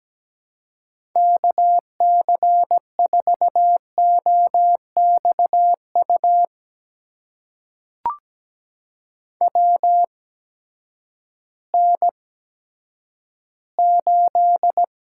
Morse code, heard as a keyed tone, about 17 wpm, spaced as for 10 wpm, 700 Hz. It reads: KC4OXU WN8
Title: USA Call Signs - 6 Characters - 8x Character Spacing - 17wpm